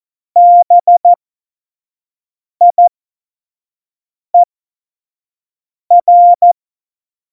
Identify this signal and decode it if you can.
Morse code, heard as a keyed tone, about 14 wpm, spaced as for 5 wpm, 700 Hz. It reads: BIER
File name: Morsekurs / Witze / Witz 07. Januar 2020, 25 F70